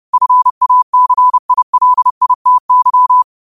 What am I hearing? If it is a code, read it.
Morse code, heard as a keyed tone, about 30 wpm, 1000 Hz. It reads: FACILITY
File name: qotdfiles / 30wpm